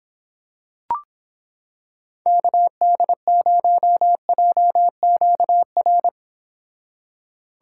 Morse code, heard as a keyed tone, about 26 wpm, 700 Hz. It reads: KD0JQR